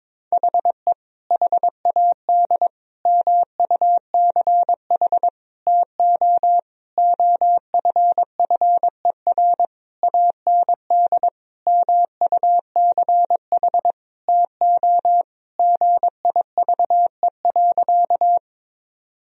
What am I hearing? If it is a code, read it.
Morse code, heard as a keyed tone, about 22 wpm, 700 Hz. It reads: HE HAD MUCH TO OFFER AND MUCH TO GIVE.
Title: Sentences from Top 1000 Words - Review Part 4 - 22wpm